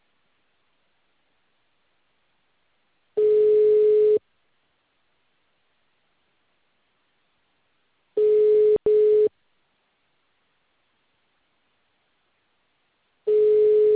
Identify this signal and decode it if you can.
Ringback tone (Europe, 425 Hz).
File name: out- -2014